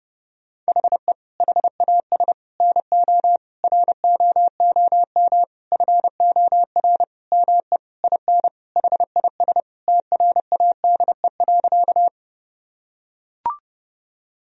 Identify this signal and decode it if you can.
Morse code, heard as a keyed tone, about 31 wpm, 700 Hz. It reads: HE HAS NO ROOM FOR ME IN HIS TRADE.
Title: Sentences from Top 1000 Words - Review Part 4 - 30wpm